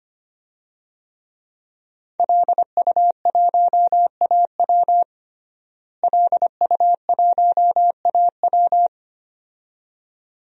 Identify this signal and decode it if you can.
Morse code, heard as a keyed tone, about 25 wpm, 700 Hz. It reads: LU1AW LU1AW
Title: DXCC-100 Call Signs - 5 Characters - Repeated 2x - 25wpm